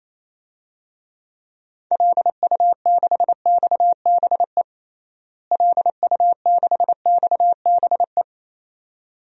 Morse code, heard as a keyed tone, about 28 wpm, 700 Hz. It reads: LU6XBE LU6XBE
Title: DXCC-100 Call Signs - 6 Characters - Repeated 2x - 28wpm